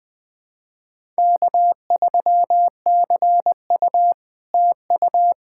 Morse code, heard as a keyed tone, about 20 wpm, 690 Hz. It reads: K3CU TU
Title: POTA QSO Practice - 20wpm